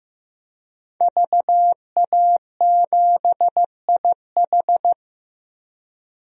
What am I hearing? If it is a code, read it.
Morse code, heard as a keyed tone, about 15 wpm, 700 Hz. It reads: VA7IH
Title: DXCC-100 Call Signs - 5 Characters - Repeated 2x - 15wpm